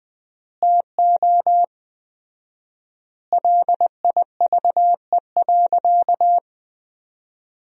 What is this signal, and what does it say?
Morse code, heard as a keyed tone, about 20 wpm, 700 Hz. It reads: TO LIVE.
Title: 4x Word-Spacing - Sentences from Top 300 Words - 20wpm